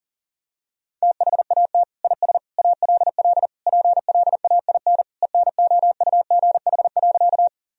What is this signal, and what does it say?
Morse code, heard as a keyed tone, about 40 wpm, 700 Hz. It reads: THAT IS ALL PLAIN ENOUGH.
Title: Sentences from Top 1000 Words - Review Part 2 - 40wpm